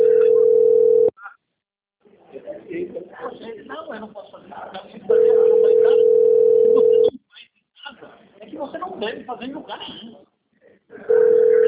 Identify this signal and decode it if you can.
Ringback tone (North America), 440 + 480 Hz, with speech.
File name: out- - - - - .28834